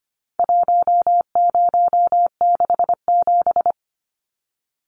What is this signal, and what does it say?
Morse code, heard as a keyed tone, about 25 wpm, 700 Hz. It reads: 1067
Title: Numbers - 4-Digit 25wpm